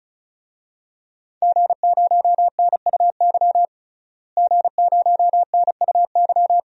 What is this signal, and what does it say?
Morse code, heard as a keyed tone, about 36 wpm, 700 Hz. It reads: G0NUY G0NUY
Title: DXCC-100 Call Signs - 5 Characters - Repeated 2x - 35wpm